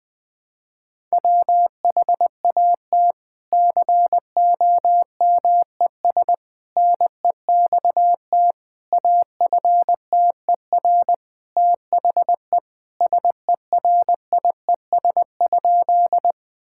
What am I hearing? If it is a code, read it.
Morse code, heard as a keyed tone, about 20 wpm, 700 Hz. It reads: WHAT COMES NEXT AFTER THE SERIES?
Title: Sentences from Top 1000 Words - Review Part 5 - 20wpm